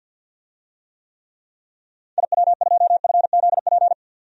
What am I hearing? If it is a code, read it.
Morse code, heard as a keyed tone, about 53 wpm, 700 Hz. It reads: IW2FZP
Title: DXCC-100 Call Signs - 6 Characters - 50wpm